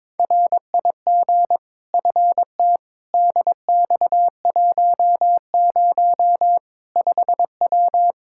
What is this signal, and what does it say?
Morse code, heard as a keyed tone, about 22 wpm, 700 Hz. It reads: RIG FT DX10 5W
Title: Practice QSO - Exchange 2 - Info - 22wpm